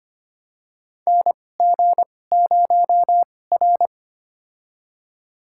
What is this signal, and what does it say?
Morse code, heard as a keyed tone, about 25 wpm, spaced as for 18 wpm, 700 Hz. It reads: NG0R
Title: USA Call Signs - 4 Characters - 2x Character Spacing - 25wpm